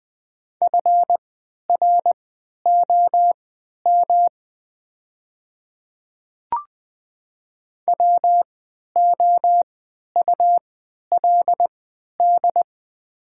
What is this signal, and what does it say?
Morse code, heard as a keyed tone, about 20 wpm, spaced as for 11 wpm, 700 Hz. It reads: FROM WOULD
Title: 3x Character Spacing - Top 200 Words - 20wpm